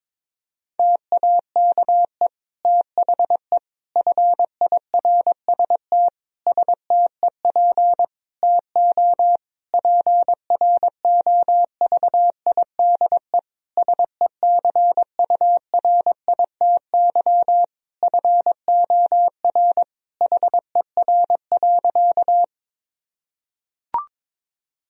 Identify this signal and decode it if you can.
Morse code, heard as a keyed tone, about 22 wpm, 700 Hz. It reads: TAKE THE FIRST STEP TO PROVIDE SECURITY FOR HER.
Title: Sentences from Top 1000 Words - Review Part 4 - 22wpm